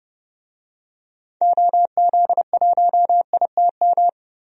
Morse code, heard as a keyed tone, about 30 wpm, 700 Hz. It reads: OZ1ITM
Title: DXCC-100 Call Signs - 6 Characters - 30wpm